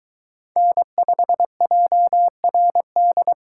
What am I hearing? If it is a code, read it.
Morse code, heard as a keyed tone, about 22 wpm, 700 Hz. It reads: N5JRD